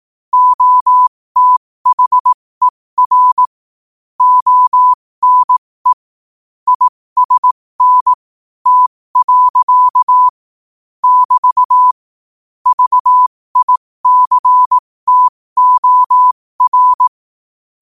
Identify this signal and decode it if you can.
Morse code, heard as a keyed tone, about 18 wpm, spaced as for 15 wpm, 1000 Hz. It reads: OTHER ONE ISN T. <BT> VICTOR
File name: qotdfiles / 15wpm